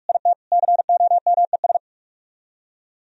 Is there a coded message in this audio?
Yes: Morse code from 0.1 to 1.8 s, heard as a keyed tone, about 48 wpm, 700 Hz. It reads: IT COMES